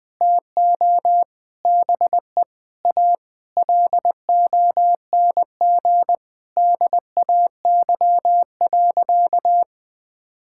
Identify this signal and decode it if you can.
Morse code, heard as a keyed tone, about 20 wpm, 700 Hz. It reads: TO BE A LONG DAY.
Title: Sentences with 7-Word Phrases - 20wpm